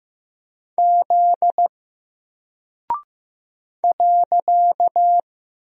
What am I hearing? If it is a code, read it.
Morse code, heard as a keyed tone, about 15 wpm, 700 Hz. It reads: Z .
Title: Single Character (Letter:Number:Punctuation:Prosign) 15wpm